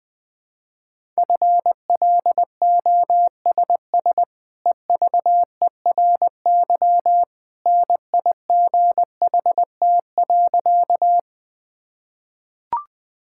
Morse code, heard as a keyed tone, about 20 wpm, 700 Hz. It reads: FLOSS EVERY NIGHT.
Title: Easy Sentences with Hints - 20wpm